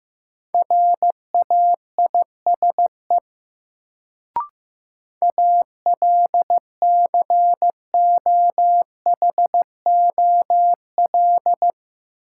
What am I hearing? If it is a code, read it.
Morse code, heard as a keyed tone, about 15 wpm, 700 Hz. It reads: RAISE ALCOHOL
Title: Sentences from Top 1400 Words - New Word Drill Alt - 15wpm